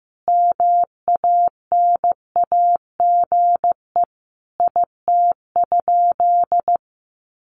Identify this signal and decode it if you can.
Morse code, heard as a keyed tone, about 15 wpm, 700 Hz. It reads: MANAGE IT?
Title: Sentences from Top 1400 Words 15wpm